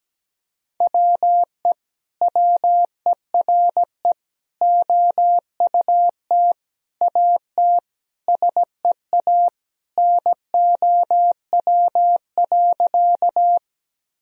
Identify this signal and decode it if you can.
Morse code, heard as a keyed tone, about 17 wpm, 700 Hz. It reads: WE WERE OUT AT SEA NOW.
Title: Sentences from Top 1000 Words - Review Part 1 - 17wpm